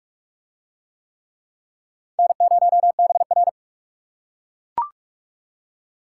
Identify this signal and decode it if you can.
Morse code, heard as a keyed tone, about 38 wpm, 700 Hz. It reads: N0BR